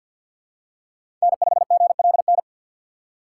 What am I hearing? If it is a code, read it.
Morse code, heard as a keyed tone, about 51 wpm, 700 Hz. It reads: N5GLN
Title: USA Call Signs - 5 Characters - 2x Character Spacing - 50wpm